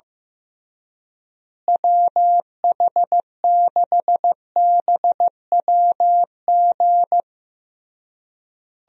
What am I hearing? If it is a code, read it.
Morse code, heard as a keyed tone, about 15 wpm, 700 Hz. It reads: WH6BWG
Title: USA Call Signs - 6 Characters - Repeated 3x - 15wpm